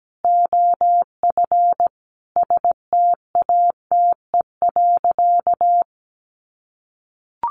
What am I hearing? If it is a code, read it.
Morse code, heard as a keyed tone, about 17 wpm, 700 Hz. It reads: OF STATE.
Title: Sentences from Top 1400 Words 17wpm